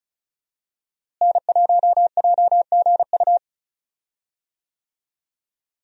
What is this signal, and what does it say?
Morse code, heard as a keyed tone, about 34 wpm, 700 Hz. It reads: N1JGU
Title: USA Call Signs - 5 Characters - Repeated 2x - 35wpm